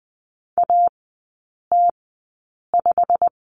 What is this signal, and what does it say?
Morse code, heard as a keyed tone, about 20 wpm, spaced as for 8 wpm, 700 Hz. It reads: AT5
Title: 093 Lesson 16e - Letter c - Three Characters 20wpm